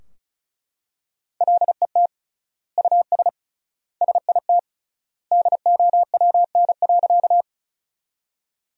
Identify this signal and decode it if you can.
Morse code, heard as a keyed tone, about 36 wpm, 700 Hz. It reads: LET US SIT DOWN.
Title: 3x Word-Spacing - Sentences from Top 500 Words - 35wpm